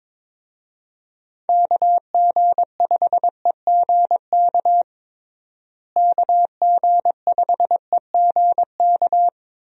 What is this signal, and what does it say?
Morse code, heard as a keyed tone, about 21 wpm, 700 Hz. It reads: KG5EGK KG5EGK
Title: USA Call Signs - 6 Characters - Repeated 2x - 22wpm